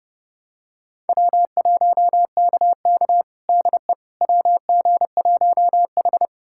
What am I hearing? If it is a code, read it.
Morse code, heard as a keyed tone, about 30 wpm, 700 Hz. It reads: W1KK DE WG1H
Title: USA Callsign DE Callsign - 4 Characters - 30wpm